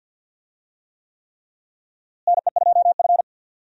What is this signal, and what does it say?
Morse code, heard as a keyed tone, about 52 wpm, 700 Hz. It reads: NE2F
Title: USA Call Signs - 4 Characters - Repeated 3x - 50wpm